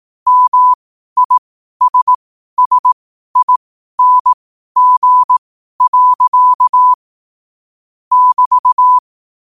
Morse code, heard as a keyed tone, about 18 wpm, spaced as for 12 wpm, 1000 Hz. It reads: MISSING. <BT>
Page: Quotes for Sun, 27 Apr 2025 in Morse Code at 12 words per minute.